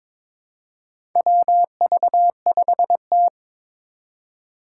Morse code, heard as a keyed tone, about 22 wpm, 700 Hz. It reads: WV5T
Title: Four Challenging Letter-Number Combinations 22wpm